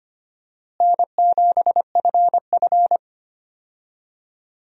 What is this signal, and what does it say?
Morse code, heard as a keyed tone, about 25 wpm, 700 Hz. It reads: N7FF
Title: USA Call Signs - 4 Characters - Repeated 2x - 25wpm